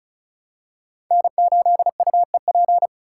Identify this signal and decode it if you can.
Morse code, heard as a keyed tone, about 36 wpm, 690 Hz. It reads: N8UEP
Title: Code Groups - 5 Letter:Numbers - 35wpm